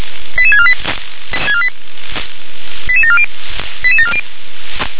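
Five-tone selective call (ZVEI), heard as quick runs of five tones, tones of about 70 ms. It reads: 88648 88740 88740
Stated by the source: SDR-received ZVEI five-tone call, code 88559.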